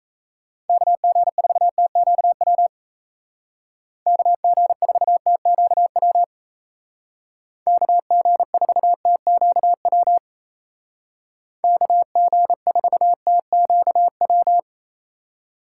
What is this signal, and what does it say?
Morse code, heard as a keyed tone, about 36 wpm, 700 Hz. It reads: KG4TQW KG4TQW KG4TQW KG4TQW